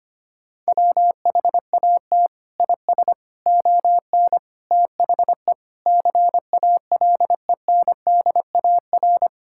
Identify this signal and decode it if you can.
Morse code, heard as a keyed tone, about 25 wpm, 700 Hz. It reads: WHAT IS ON THE CALENDAR
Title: Sentences from Top 1600 Words 25wpm